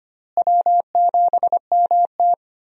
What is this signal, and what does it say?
Morse code, heard as a keyed tone, about 25 wpm, 700 Hz. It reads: W7MT
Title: USA Call Signs - 4 Characters - Repeated 2x - 25wpm